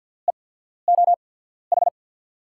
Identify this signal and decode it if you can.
Morse code, heard as a keyed tone, about 47 wpm, spaced as for 14 wpm, 700 Hz. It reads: EOH
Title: USA Call Signs - 6 Characters - 8x Character Spacing - 50wpm